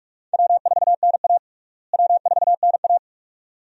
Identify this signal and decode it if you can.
Morse code, heard as a keyed tone, about 46 wpm, 700 Hz. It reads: W4NA W4NA
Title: USA Call Signs - 4 Characters - Repeated 2x - 45wpm